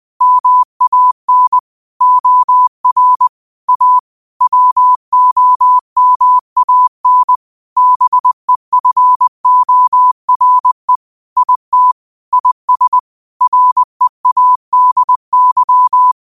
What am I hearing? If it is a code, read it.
Morse code, heard as a keyed tone, about 20 wpm, 1000 Hz. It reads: MAN OR A WOMAN BEFORE IT IS READY
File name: qotdfiles / 20wpm